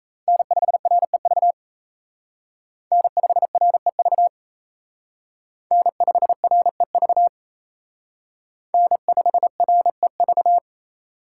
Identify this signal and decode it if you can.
Morse code, heard as a keyed tone, about 35 wpm, 700 Hz. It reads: N5REV N5REV N5REV N5REV
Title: Speed Racing - USA 1x3 Call Signs 28wpm